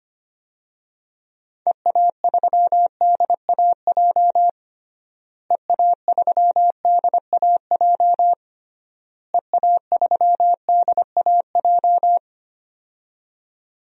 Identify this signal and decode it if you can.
Morse code, heard as a keyed tone, about 25 wpm, 700 Hz. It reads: EA3DAJ EA3DAJ EA3DAJ
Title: DXCC-100 Call Signs - 6 Characters - Repeated 3x - 25wpm